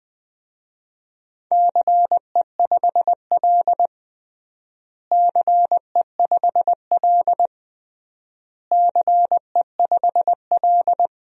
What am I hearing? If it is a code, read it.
Morse code, heard as a keyed tone, about 20 wpm, 700 Hz. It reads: CE5L CE5L CE5L
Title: DXCC-100 Call Signs - 4 Characters - Repeated 3x - 20wpm